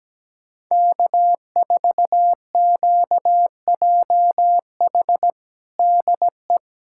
Morse code, heard as a keyed tone, about 17 wpm, 700 Hz. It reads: K4QJH DE
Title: Practice QSO - Exchange 4 - Ending - 17wpm